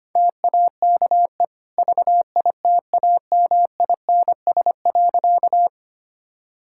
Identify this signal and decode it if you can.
Morse code, heard as a keyed tone, about 25 wpm, 690 Hz. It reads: TAKE VITAMINS.
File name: Easy Sentences with Hints - 25wpm